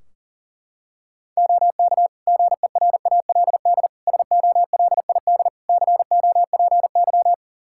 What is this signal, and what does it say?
Morse code, heard as a keyed tone, about 40 wpm, 700 Hz. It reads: OK GERALD SOLID COPY
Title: Practice QSO - Exchange 3 - More Info - 40wpm